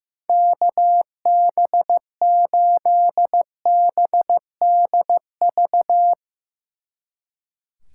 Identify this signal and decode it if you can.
Morse code, heard as a keyed tone, about 16 wpm, 700 Hz. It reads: KB8BDV
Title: Speed Racing - USA 2x3 Call Signs 15wpm